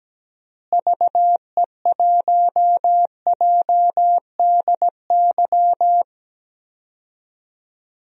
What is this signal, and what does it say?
Morse code, heard as a keyed tone, about 17 wpm, 700 Hz. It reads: VE1JDY
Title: DXCC-100 Call Signs - 6 Characters - Repeated 2x - 17wpm